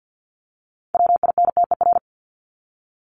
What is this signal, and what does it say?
Morse code, heard as a keyed tone, about 50 wpm, 700 Hz. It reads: WINTER
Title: Top 1501-1750 Words 50wpm